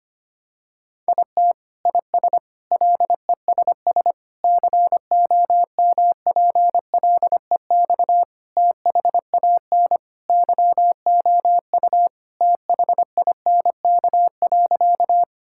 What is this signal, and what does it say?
Morse code, heard as a keyed tone, about 25 wpm, 700 Hz. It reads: IT IS LESS COMPLEX THAN YOU THINK.
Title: Sentences from Top 1000 Words - Review Part 6 - 25wpm